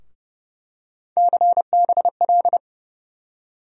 Morse code, heard as a keyed tone, about 32 wpm, 700 Hz. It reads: CBL
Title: USA Call Sign Postfixes - 3 Letters 30wpm